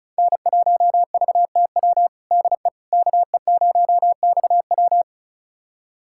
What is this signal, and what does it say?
Morse code, heard as a keyed tone, about 36 wpm, 700 Hz. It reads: N1VTW DE KE0XW
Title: USA Callsign DE Callsign - 5 Characters - 35wpm